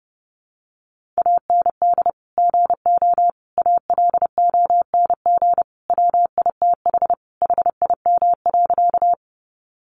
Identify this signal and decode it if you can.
Morse code, heard as a keyed tone, about 30 wpm, 700 Hz. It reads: AND GO ALONG WITH HIM.
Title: Sentences from Top 700 Words 30wpm